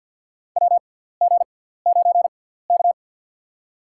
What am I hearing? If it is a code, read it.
Morse code, heard as a keyed tone, about 51 wpm, spaced as for 17 wpm, 700 Hz. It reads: WG9K
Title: USA Call Signs - 6x Character Spacing - 50wpm